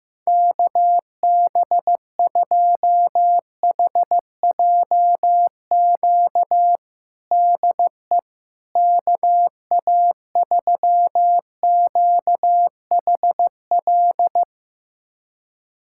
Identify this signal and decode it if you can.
Morse code, heard as a keyed tone, about 15 wpm, 700 Hz. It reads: KB2HJQ DE KA3QHL
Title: USA Callsign DE Callsign 15wpm